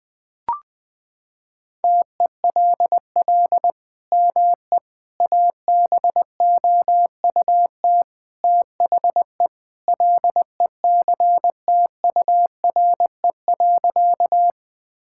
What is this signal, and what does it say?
Morse code, heard as a keyed tone, about 20 wpm, 700 Hz. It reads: TELL ME ABOUT THE LECTURE.
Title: Sentences from Top 1600 Words 20wpm